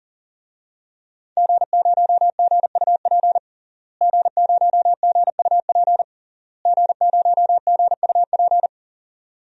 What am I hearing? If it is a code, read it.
Morse code, heard as a keyed tone, about 40 wpm, 700 Hz. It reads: G0GUP G0GUP G0GUP
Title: DXCC-100 Call Signs - 5 Characters - Repeated 3x - 40wpm